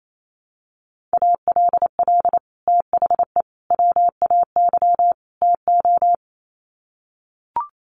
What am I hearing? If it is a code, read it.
Morse code, heard as a keyed tone, about 28 wpm, 700 Hz. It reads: ALL THE WAY TO
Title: Sets of 4 Words 28wpm